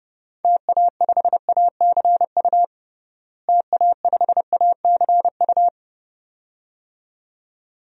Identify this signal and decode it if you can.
Morse code, heard as a keyed tone, about 30 wpm, 700 Hz. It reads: TA5ACU TA5ACU
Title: DXCC-100 Call Signs - 6 Characters - Repeated 2x - 30wpm